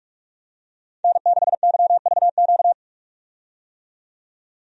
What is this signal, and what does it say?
Morse code, heard as a keyed tone, about 44 wpm, 700 Hz. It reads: N6YVQ
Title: USA Call Signs - 5 Characters - 45wpm